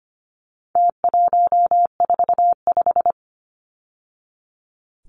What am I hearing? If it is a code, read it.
Morse code, heard as a keyed tone, about 25 wpm, 700 Hz. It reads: T145
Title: Numbers - Contest Style - 4-Digit 25wpm